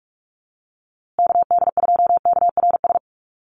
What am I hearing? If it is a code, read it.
Morse code, heard as a keyed tone, about 44 wpm, 700 Hz. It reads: KD2KRS
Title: USA Call Signs Repeated 2x 45wpm